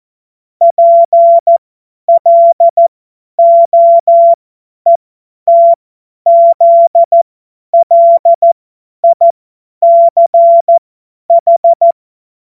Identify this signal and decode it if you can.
Morse code, heard as a keyed tone, about 14 wpm, spaced as for 10 wpm, 680 Hz. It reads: PLOETZLICH
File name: Morsekurs / Witze / Witz 14. Februar 2023 50 F70